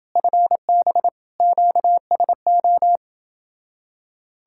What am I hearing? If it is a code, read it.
Morse code, heard as a keyed tone, about 29 wpm, 700 Hz. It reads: FB QSO